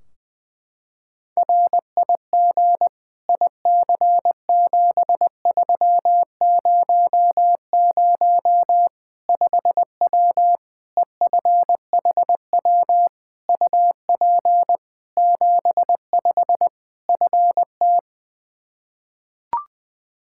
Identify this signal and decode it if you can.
Morse code, heard as a keyed tone, about 20 wpm, 700 Hz. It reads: RIG IC7300 5W EFHW UP 75 FT
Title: Practice QSO - Exchange 2 - Info - 20wpm